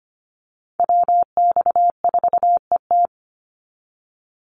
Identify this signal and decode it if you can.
Morse code, heard as a keyed tone, about 25 wpm, 700 Hz. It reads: WX4ET
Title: NA SOTA Chaser Callsigns 25wpm